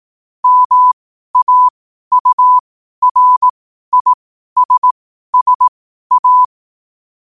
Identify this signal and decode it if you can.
Morse code, heard as a keyed tone, about 18 wpm, spaced as for 12 wpm, 1000 Hz. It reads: MAURISSA
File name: qotdfiles / 12wpm